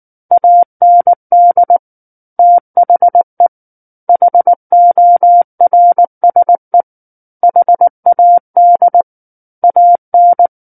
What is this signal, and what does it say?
Morse code, heard as a keyed tone, about 19 wpm, 700 Hz. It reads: AND THE HORSE HAD AN
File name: Book One / 10-Chapter 10 - 19 WPM